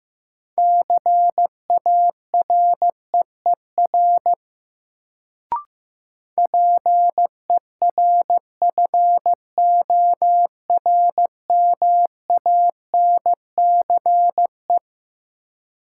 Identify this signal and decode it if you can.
Morse code, heard as a keyed tone, about 15 wpm, 700 Hz. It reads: CAREER PERFORMANCE
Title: Sentences from Top 1000 Words - New Word Drill - 15wpm